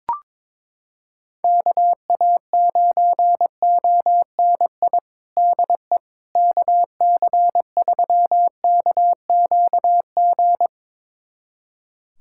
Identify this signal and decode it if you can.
Morse code, heard as a keyed tone, about 22 wpm, 700 Hz. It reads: KA9ONI DE KC3KQG
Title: USA Callsign DE Callsign - 6 Characters - 22wpm